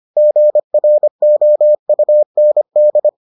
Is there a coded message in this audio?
Yes: Morse code from 0.2 to 3.1 s, heard as a keyed tone, about 25 wpm, 600 Hz. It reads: GROUND